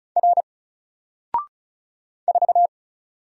Morse code, heard as a keyed tone, about 36 wpm, 700 Hz. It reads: R 4
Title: Single Character (Letter:Number:Punctuation:Prosign) 35wpm